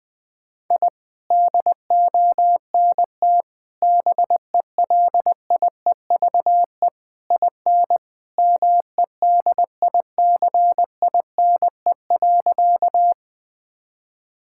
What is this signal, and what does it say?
Morse code, heard as a keyed tone, about 21 wpm, 700 Hz. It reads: I DONT BELIEVE IN MEDICINE.
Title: Sentences from Top 1000 Words - Review Part 6 - 20wpm